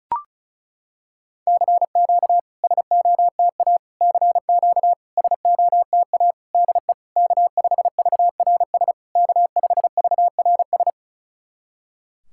Morse code, heard as a keyed tone, about 36 wpm, 700 Hz. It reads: CQ SOTA CQ SOTA DE K5VRS K5VRS
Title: SOTA QSO Practice 35wpm